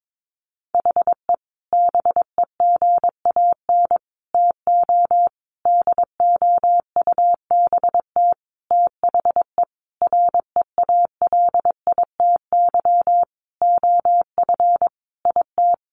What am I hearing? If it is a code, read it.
Morse code, heard as a keyed tone, about 22 wpm, 700 Hz. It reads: HE BEGAN TO DOUBT THE REALITY OF IT
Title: Sentences from Top 1500 Words 22wpm